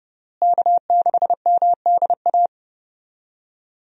Morse code, heard as a keyed tone, about 32 wpm, 700 Hz. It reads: K6MDA
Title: USA Call Signs - 5 Characters - Repeated 3x - 30wpm